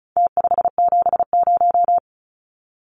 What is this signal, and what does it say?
Morse code, heard as a keyed tone, about 34 wpm, 700 Hz. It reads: T570
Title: Numbers - Contest Style - 4-Digit 35wpm